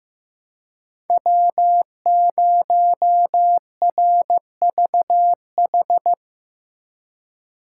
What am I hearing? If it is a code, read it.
Morse code, heard as a keyed tone, about 15 wpm, 700 Hz. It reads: W0RVH
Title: USA Call Signs - 2x Character Spacing - 15wpm